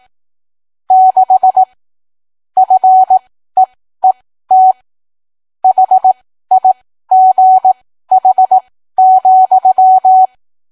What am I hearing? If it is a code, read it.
Morse code, heard as a keyed tone, about 18 wpm, spaced as for 13 wpm, 750 Hz. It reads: 6 FEET HIGH,